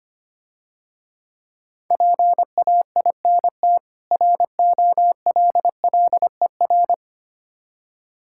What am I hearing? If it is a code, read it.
Morse code, heard as a keyed tone, about 25 wpm, 700 Hz. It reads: PAINT ROLLER
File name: practice / 25wpm / Difficult Words - Speed-Racing with Hints - 25wpm